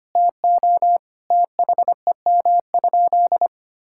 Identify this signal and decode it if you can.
Morse code, heard as a keyed tone, about 25 wpm, 700 Hz. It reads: TO THEM?
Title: Sentences from Top 1000 Words - Review Part 5 - 25wpm